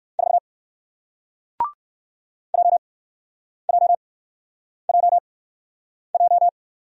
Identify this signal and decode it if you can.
Morse code, heard as a keyed tone, about 58 wpm, spaced as for 9 wpm, 700 Hz. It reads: 4 JJJJ